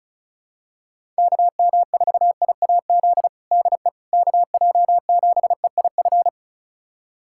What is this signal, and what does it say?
Morse code, heard as a keyed tone, about 34 wpm, 700 Hz. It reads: KM4IAZ DE KJ7EIF
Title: USA Callsign DE Callsign 35wpm